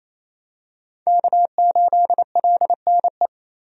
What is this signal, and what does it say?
Morse code, heard as a keyed tone, about 28 wpm, 700 Hz. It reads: K8LNE
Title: USA Call Signs - 5 Characters - 28wpm